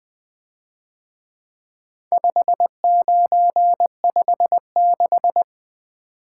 Morse code, heard as a keyed tone, about 23 wpm, 700 Hz. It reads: 5956
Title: Speed-Racing - Four Letter-Number Combinations - 20wpm